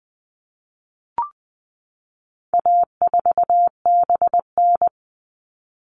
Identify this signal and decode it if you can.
Morse code, heard as a keyed tone, about 20 wpm, 700 Hz. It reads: A4BN
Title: All Cut Numbers - 4-Digit 20wpm